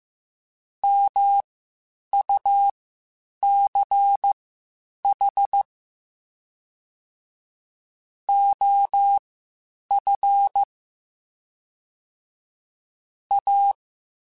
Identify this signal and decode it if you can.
Morse code, heard as a keyed tone, about 15 wpm, spaced as for 8 wpm, 800 Hz. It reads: MUCH OF A
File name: (spaced) / QSOs Spaced-mp3s / WD0BGP 1 @13 sp